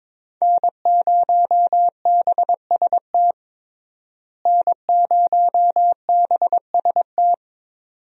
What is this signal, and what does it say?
Morse code, heard as a keyed tone, about 22 wpm, 700 Hz. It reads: N0BST N0BST
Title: USA Call Signs - 5 Characters - Repeated 2x - 22wpm